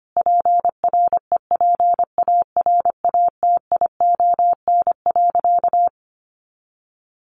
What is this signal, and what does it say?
Morse code, heard as a keyed tone, about 25 wpm, 700 Hz. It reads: PREPARATION.
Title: Sentences from Top 1400 Words 25wpm